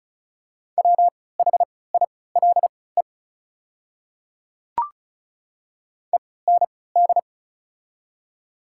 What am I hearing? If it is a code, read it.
Morse code, heard as a keyed tone, about 34 wpm, spaced as for 20 wpm, 700 Hz. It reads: WHILE END
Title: 3x Character Spacing - Top 300 Words - 35wpm